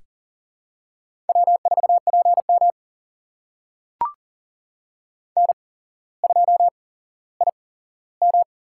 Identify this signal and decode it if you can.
Morse code, heard as a keyed tone, about 42 wpm, spaced as for 17 wpm, 700 Hz. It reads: W4PM N2IM
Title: USA Call Signs - 4 Characters - 8x Character Spacing - 40wpm